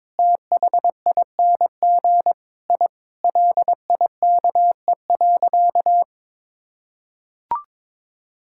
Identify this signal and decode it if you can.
Morse code, heard as a keyed tone, about 22 wpm, 700 Hz. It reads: THING I LIKE.
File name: practice / 22wpm / Sentences with 3-Word Phrases - 22wpm